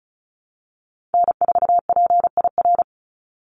Morse code, heard as a keyed tone, about 34 wpm, 700 Hz. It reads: N4PIR
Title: Call Signs - 35wpm